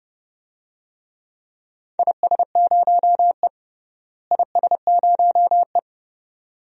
Morse code, heard as a keyed tone, about 29 wpm, 700 Hz. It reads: IS0E IS0E
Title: DXCC-100 Call Signs - 4 Characters - Repeated 2x - 30wpm